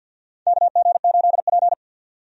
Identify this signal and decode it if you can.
Morse code, heard as a keyed tone, about 50 wpm, 700 Hz. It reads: KG8P
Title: USA Call Signs - 4 Characters - 50wpm